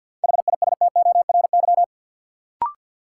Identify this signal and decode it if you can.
Morse code, heard as a keyed tone, about 52 wpm, 700 Hz. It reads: HISTORY
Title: Sentences from Top 600 Words - New Word Drill - 50wpm